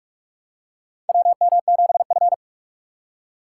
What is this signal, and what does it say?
Morse code, heard as a keyed tone, about 46 wpm, 700 Hz. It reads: WM7F
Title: USA Call Signs - 4 Characters - 2x Character Spacing - 45wpm